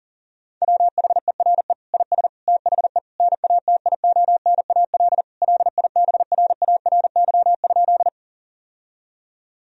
Morse code, heard as a keyed tone, about 40 wpm, 700 Hz. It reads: WHERE IS THE NATIONAL LIBRARY?
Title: Sentences from Top 1000 Words - Review Part 4 - 40wpm